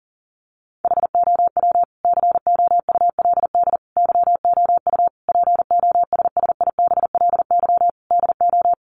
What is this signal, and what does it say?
Morse code, heard as a keyed tone, about 40 wpm, 700 Hz. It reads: HOW COULD YOU POSSIBLY DO
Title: Sentences from Top 1300 Words 40wpm